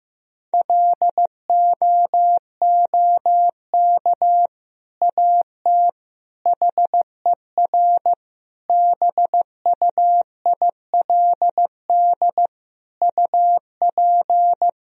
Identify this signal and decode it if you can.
Morse code, heard as a keyed tone, about 15 wpm, 700 Hz. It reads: LOOK AT HER BUILD UP
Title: Sentences from Top 1000 Words - Review Part 1 - 15wpm